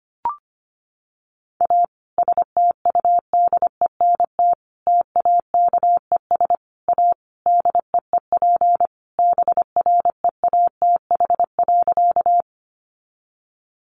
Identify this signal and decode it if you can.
Morse code, heard as a keyed tone, about 25 wpm, 700 Hz. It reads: A STUDENT TAKES A DEEP BREATH.
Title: Additional Easy Phrases 25wpm